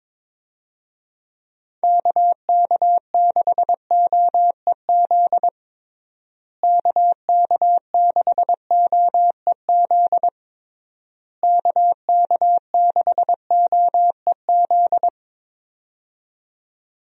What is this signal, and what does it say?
Morse code, heard as a keyed tone, about 22 wpm, 700 Hz. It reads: KK6OEZ KK6OEZ KK6OEZ
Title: USA Call Signs - 6 Characters - Repeated 3x - 22wpm